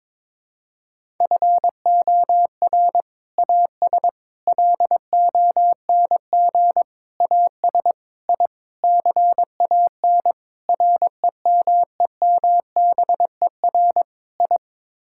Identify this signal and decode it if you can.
Morse code, heard as a keyed tone, about 22 wpm, 700 Hz. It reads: FOR AS LONG AS I CAN REMEMBER I
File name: practice / 22wpm / Sets of 8 Words - 22wpm